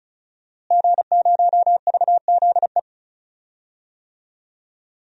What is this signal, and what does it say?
Morse code, heard as a keyed tone, about 34 wpm, 700 Hz. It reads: G0VZE
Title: DXCC-100 Call Signs - 5 Characters - Repeated 3x - 35wpm